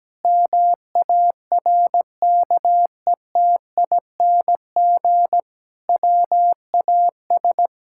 Morse code, heard as a keyed tone, about 17 wpm, 700 Hz. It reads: MARKETING WAS
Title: Sentences from Top 1000 Words - Review Part 5 - 17wpm